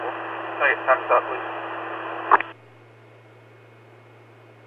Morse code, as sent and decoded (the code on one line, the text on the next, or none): none